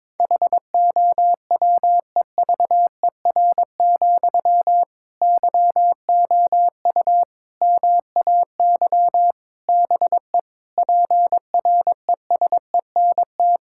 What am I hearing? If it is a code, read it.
Morse code, heard as a keyed tone, about 22 wpm, 700 Hz. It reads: HOWEVER, YOU MAY BE PRESENT
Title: Sentences from Top 1600 Words 22wpm